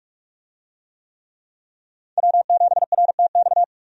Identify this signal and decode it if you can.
Morse code, heard as a keyed tone, about 43 wpm, 700 Hz. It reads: W7RTX
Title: USA Call Signs - 5 Characters - 45wpm